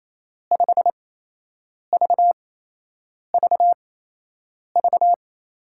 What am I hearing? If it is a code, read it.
Morse code, heard as a keyed tone, about 28 wpm, spaced as for 8 wpm, 700 Hz. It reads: 5VVV
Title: USA Call Signs - 5 Characters - 8x Character Spacing - 28wpm